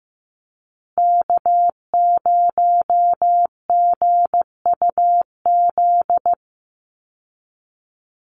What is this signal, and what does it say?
Morse code, heard as a keyed tone, about 15 wpm, 700 Hz. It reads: K0GUZ
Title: Call Signs Repeated 2x - 15wpm